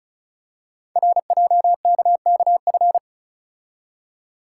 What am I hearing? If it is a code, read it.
Morse code, heard as a keyed tone, about 36 wpm, 690 Hz. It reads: RJKKF
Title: Code Groups - 5 Letters - 35wpm